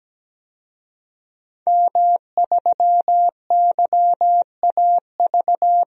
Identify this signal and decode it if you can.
Morse code, heard as a keyed tone, about 17 wpm, 700 Hz. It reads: M3YAV
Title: DXCC-100 Call Signs - 5 Characters - 17wpm